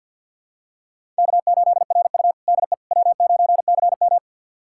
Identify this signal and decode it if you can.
Morse code, heard as a keyed tone, about 51 wpm, 700 Hz. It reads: K8RU DE W9CM
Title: USA Callsign DE Callsign - 4 Characters - 50wpm